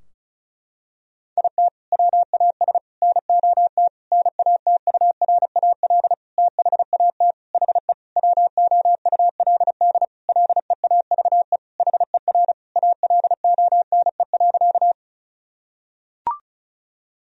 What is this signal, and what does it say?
Morse code, heard as a keyed tone, about 34 wpm, 700 Hz. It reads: IT WAS NOT NATURAL THAT HE WOULD LEAVE HER ALONE.
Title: Sentences from Top 1000 Words - Review Part 4 - 35wpm